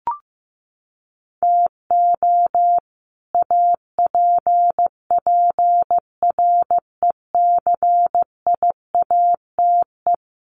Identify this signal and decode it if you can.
Morse code, heard as a keyed tone, about 15 wpm, 700 Hz. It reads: TO APPRECIATE
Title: Sentences from Top 1300 Words 15wpm